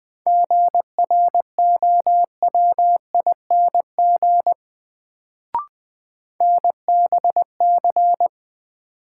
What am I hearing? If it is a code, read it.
Morse code, heard as a keyed tone, about 20 wpm, 700 Hz. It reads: GROWING NBC
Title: Sets of 5 Words Encore - New Word Drill - 20wpm